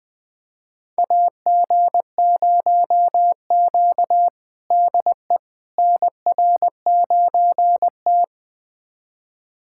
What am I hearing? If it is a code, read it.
Morse code, heard as a keyed tone, about 20 wpm, 700 Hz. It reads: AG0Q DE NR9T
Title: USA Callsign DE Callsign - 4 Characters - 20wpm